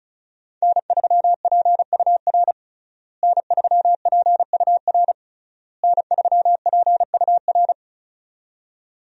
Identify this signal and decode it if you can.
Morse code, heard as a keyed tone, about 36 wpm, 700 Hz. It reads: N3PUR N3PUR N3PUR
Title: USA Call Signs - 5 Characters - Repeated 3x - 35wpm